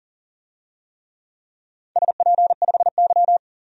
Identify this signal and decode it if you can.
Morse code, heard as a keyed tone, about 40 wpm, 700 Hz. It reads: SP5Y
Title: DXCC-100 Call Signs - 4 Characters - 40wpm